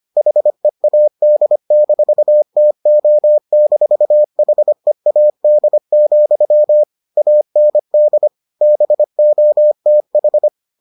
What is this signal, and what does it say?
Morse code, heard as a keyed tone, about 25 wpm, 600 Hz. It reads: HEAD-TO-HEAD, AND BOTH